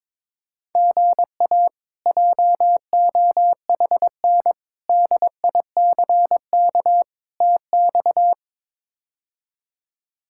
Morse code, heard as a keyed tone, about 22 wpm, 700 Hz. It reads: GA JOHN DICK TX
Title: K1USN SST QSO Practice - 22wpm